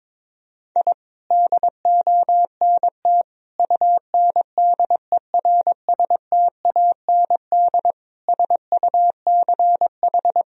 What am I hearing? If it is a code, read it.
Morse code, heard as a keyed tone, about 22 wpm, 700 Hz. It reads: I DONT UNDERSTAND SUCH
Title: Sentences from Top 1000 Words - Review Part 3 - 22wpm